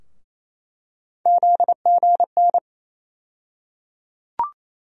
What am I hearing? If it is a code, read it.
Morse code, heard as a keyed tone, about 27 wpm, 700 Hz. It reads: ZGN
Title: USA Call Sign Postfixes - 3 Letters 28wpm